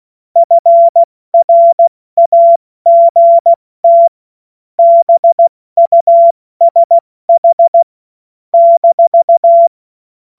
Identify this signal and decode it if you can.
Morse code, heard as a keyed tone, about 16 wpm, spaced as for 14 wpm, 680 Hz. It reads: FRAGT BUSH -
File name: Morsekurs / Witze / Witz 09. Februar 2021, 70 F80